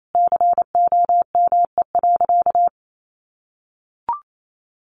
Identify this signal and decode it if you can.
Morse code, heard as a keyed tone, about 28 wpm, 700 Hz. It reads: COME.
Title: Sentences from Top 700 Words 28wpm